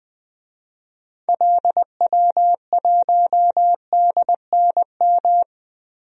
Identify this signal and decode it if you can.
Morse code, heard as a keyed tone, about 21 wpm, 700 Hz. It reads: LW1DNM
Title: DXCC-100 Call Signs - 6 Characters - 20wpm